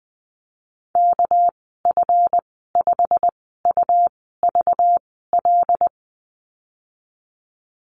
Morse code, heard as a keyed tone, about 19 wpm, spaced as for 14 wpm, 700 Hz. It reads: KF5UVL
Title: 238 Lesson 38g - Letter q - Call Signs 20wpm